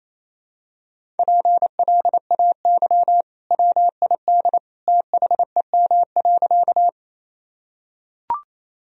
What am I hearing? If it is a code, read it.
Morse code, heard as a keyed tone, about 28 wpm, 700 Hz. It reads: PLAY WID THEM.
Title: Sentences from Top 200 Words and Abbreviations 28wpm